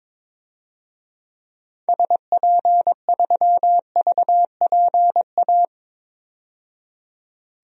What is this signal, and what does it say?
Morse code, heard as a keyed tone, about 22 wpm, 700 Hz. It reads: SP3VPA
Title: DXCC-100 Call Signs - 6 Characters - 22wpm